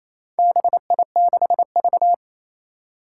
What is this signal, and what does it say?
Morse code, heard as a keyed tone, about 28 wpm, 700 Hz. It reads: BI6V
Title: Four Challenging Letter-Number Combinations 28wpm